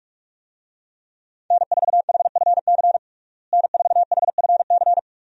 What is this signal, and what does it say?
Morse code, heard as a keyed tone, about 41 wpm, 700 Hz. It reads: N4HFC N4HFC
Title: USA Call Signs - 5 Characters - Repeated 3x - 45wpm